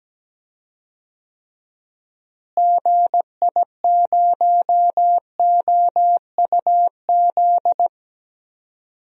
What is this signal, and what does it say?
Morse code, heard as a keyed tone, about 17 wpm, 700 Hz. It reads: GI0OUZ
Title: DXCC-100 Call Signs - 6 Characters - Repeated 3x - 17wpm